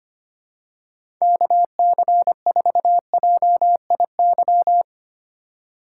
Morse code, heard as a keyed tone, about 25 wpm, 700 Hz. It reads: KC4JIY
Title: USA Call Signs - 6 Characters - 4x Character Spacing - 25wpm